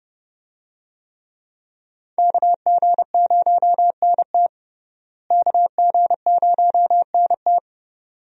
Morse code, heard as a keyed tone, about 31 wpm, 700 Hz. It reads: KG0NT KG0NT
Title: USA Call Signs - 5 Characters - Repeated 2x - 30wpm